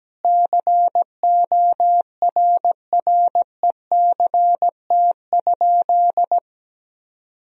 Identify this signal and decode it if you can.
Morse code, heard as a keyed tone, about 17 wpm, 700 Hz. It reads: CORRECT?
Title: Sentences from Top 1000 Words - Review Part 4 - 17wpm